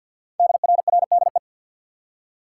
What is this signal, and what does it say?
Morse code, heard as a keyed tone, about 48 wpm, 700 Hz. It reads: DRRDE
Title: Code Groups - 5 Letters - 50wpm